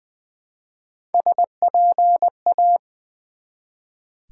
Morse code, heard as a keyed tone, about 20 wpm, 700 Hz. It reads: SPA
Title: Build A Word - 1 Letter At A Time - 4-Letter Words - 20wpm